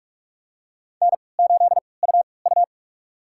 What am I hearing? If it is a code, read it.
Morse code, heard as a keyed tone, about 46 wpm, spaced as for 26 wpm, 700 Hz. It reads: N8UU
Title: USA Call Signs - 4 Characters - 3x Character Spacing - 45wpm